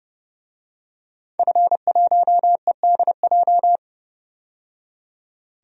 Morse code, heard as a keyed tone, about 30 wpm, 700 Hz. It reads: F1EDJ
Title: DXCC-100 Call Signs - 5 Characters - 30wpm